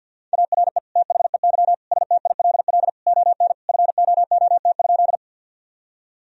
Morse code, heard as a keyed tone, about 50 wpm, 700 Hz. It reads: ARE THEY STILL ON FOOT?
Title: Sentences from Top 1000 Words - Review Part 3 - 50wpm